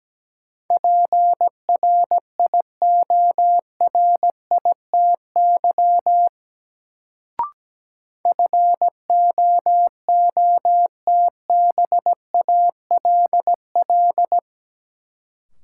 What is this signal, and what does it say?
Morse code, heard as a keyed tone, about 17 wpm, 700 Hz. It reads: PRIORITY FOOTBALL
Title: Sentences from Top 1600 Words - New Word Drill - 17wpm